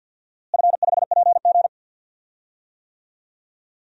Morse code, heard as a keyed tone, about 47 wpm, 700 Hz. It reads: F5PG
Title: DXCC-100 Call Signs - 4 Characters - Repeated 2x - 50wpm